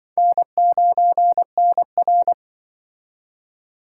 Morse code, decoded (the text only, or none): N9NR